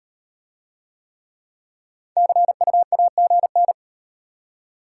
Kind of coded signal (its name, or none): Morse code